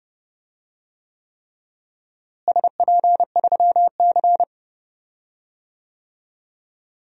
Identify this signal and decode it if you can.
Morse code, heard as a keyed tone, about 31 wpm, 700 Hz. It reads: SP3C
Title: DXCC-100 Call Signs - 4 Characters - Repeated 3x - 30wpm